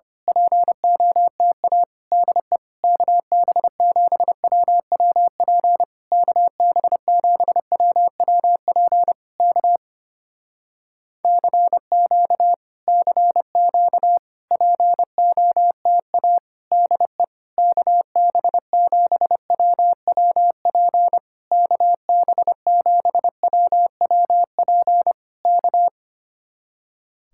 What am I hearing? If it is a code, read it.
Morse code, heard as a keyed tone, about 27 wpm, 700 Hz. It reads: POTA DE KB7WWP KB7WWP K CQ CQ POTA DE KB7WWP KB7WWP K
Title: Speed-Racing Common QSO Phrases - 25wpm